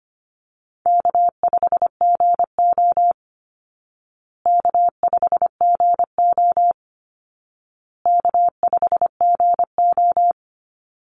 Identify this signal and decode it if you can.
Morse code, heard as a keyed tone, about 25 wpm, 700 Hz. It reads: K5GO K5GO K5GO
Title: Call Signs Repeated 2x - 25wpm